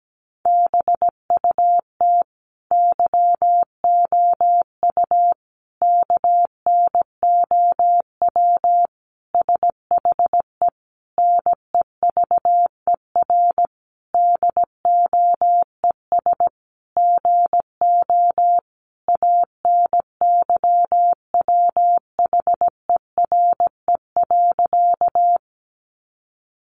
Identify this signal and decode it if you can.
Morse code, heard as a keyed tone, about 17 wpm, 700 Hz. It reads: BUT YOU KNOW SHE NEVER DOES GO ANYWHERE.
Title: Sentences from Top 1400 Words 17wpm